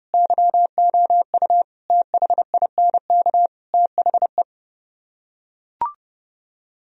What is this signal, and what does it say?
Morse code, heard as a keyed tone, about 30 wpm, 700 Hz. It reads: YOU THINK THE